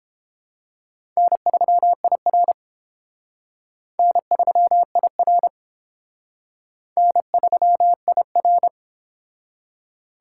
Morse code, heard as a keyed tone, about 30 wpm, 700 Hz. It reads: N3IR N3IR N3IR